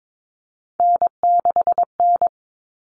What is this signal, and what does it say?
Morse code, heard as a keyed tone, about 22 wpm, 700 Hz. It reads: N6N
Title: All Cut Numbers - 3-Digit 22wpm